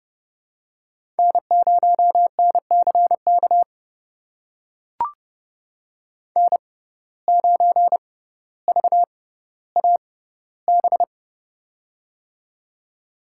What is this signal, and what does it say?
Morse code, heard as a keyed tone, about 31 wpm, spaced as for 16 wpm, 700 Hz. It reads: N0NCK N9VAB
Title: USA Call Signs - 5 Characters - 6x Character Spacing - 30wpm